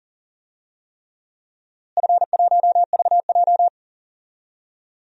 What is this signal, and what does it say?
Morse code, heard as a keyed tone, about 37 wpm, 700 Hz. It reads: F1VJ
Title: DXCC-100 Call Signs - 4 Characters - Repeated 3x - 40wpm